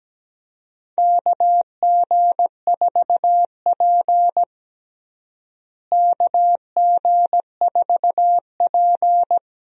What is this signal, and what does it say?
Morse code, heard as a keyed tone, about 17 wpm, 700 Hz. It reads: KG4P KG4P
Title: USA Call Signs - 4 Characters - Repeated 3x - 17wpm